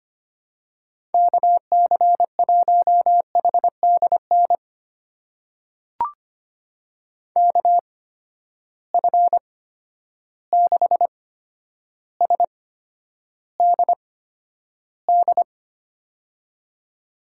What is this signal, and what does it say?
Morse code, heard as a keyed tone, about 25 wpm, spaced as for 11 wpm, 700 Hz. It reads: KC1HDN KF6SDD
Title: USA Call Signs - 6 Characters - 8x Character Spacing - 25wpm